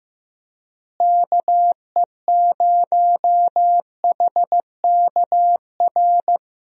Morse code, heard as a keyed tone, about 16 wpm, 700 Hz. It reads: KE0HKR
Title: USA Call Signs - 6 Characters - Repeated 3x - 15wpm